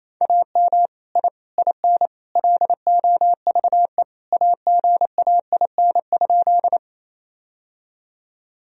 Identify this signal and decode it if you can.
Morse code, heard as a keyed tone, about 28 wpm, 700 Hz. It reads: AM I IN LOVE AGAIN?
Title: Sentences from Top 1000 Words - Review Part 2 - 28wpm